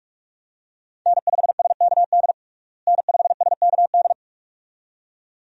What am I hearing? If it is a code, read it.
Morse code, heard as a keyed tone, about 48 wpm, 700 Hz. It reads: N5SKD N5SKD
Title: USA Call Signs - 5 Characters - Repeated 2x - 45wpm